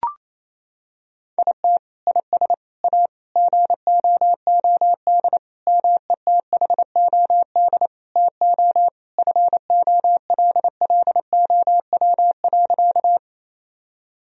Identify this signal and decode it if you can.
Morse code, heard as a keyed tone, about 28 wpm, 700 Hz. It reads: IT IS A GOOD METHOD TO FOLLOW.
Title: Sentences from Top 1000 Words - Review Part 3 - 28wpm